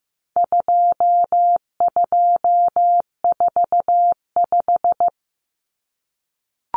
Morse code, heard as a keyed tone, about 15 wpm, 700 Hz. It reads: 2245
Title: Numbers - 4-Digit 15wpm